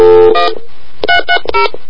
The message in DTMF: 1 330